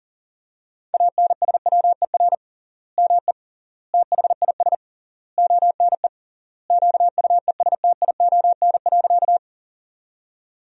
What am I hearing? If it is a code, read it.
Morse code, heard as a keyed tone, about 37 wpm, 700 Hz. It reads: ANSWER ME THIS ONE QUESTION.
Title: 3x Word-Spacing - Sentences from Top 400 Words - 40wpm